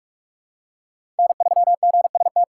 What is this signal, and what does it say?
Morse code, heard as a keyed tone, about 46 wpm, 700 Hz. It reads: N3GST
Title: USA Call Signs - 5 Characters - 3x Character Spacing - 45wpm